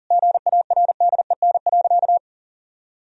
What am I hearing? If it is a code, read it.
Morse code, heard as a keyed tone, about 40 wpm, 700 Hz. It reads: GARDEN.
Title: Sentences from Top 1000 Words - Review Part 4 - 40wpm